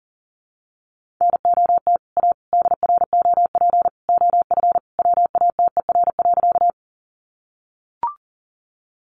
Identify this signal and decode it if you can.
Morse code, heard as a keyed tone, about 40 wpm, 700 Hz. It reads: NOT A DROP OF WATER.
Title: Sentences from Top 1400 Words 40wpm